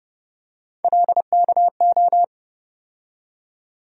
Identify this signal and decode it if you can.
Morse code, heard as a keyed tone, about 31 wpm, 700 Hz. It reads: LKO
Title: USA Call Sign Postfixes - 3 Letters 30wpm